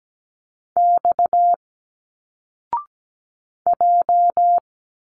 Morse code, heard as a keyed tone, about 17 wpm, 700 Hz. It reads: X J
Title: Single Letters 17wpm